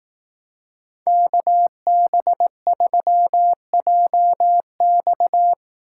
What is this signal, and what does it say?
Morse code, heard as a keyed tone, about 18 wpm, 700 Hz. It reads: KB3JX
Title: Speed Racing - USA 2x2 Call Signs 15wpm